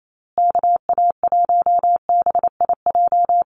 Morse code, heard as a keyed tone, about 28 wpm, 700 Hz. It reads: KA1BIJ
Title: USA Call Signs Repeated 2x 28wpm